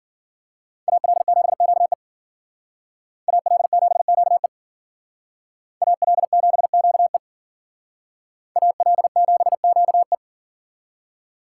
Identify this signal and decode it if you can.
Morse code, heard as a keyed tone, about 50 wpm, 700 Hz. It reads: AL7QE AL7QE AL7QE AL7QE
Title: Speed Racing - USA 2x2 Call Signs 40wpm